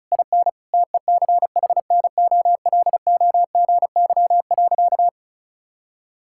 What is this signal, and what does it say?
Morse code, heard as a keyed tone, about 36 wpm, 700 Hz. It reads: IN TECHNOLOGY.
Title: Sentences from Top 1000 Words - Review Part 5 - 35wpm